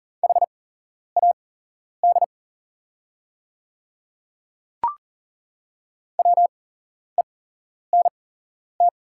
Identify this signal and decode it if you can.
Morse code, heard as a keyed tone, about 40 wpm, spaced as for 11 wpm, 700 Hz. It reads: HAD WENT
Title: 8x Character Spacing - Top 200 Words - 40wpm